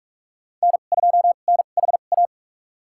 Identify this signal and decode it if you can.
Morse code, heard as a keyed tone, about 46 wpm, spaced as for 33 wpm, 700 Hz. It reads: N2NHA
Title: USA Call Signs - 5 Characters - 2x Character Spacing - 45wpm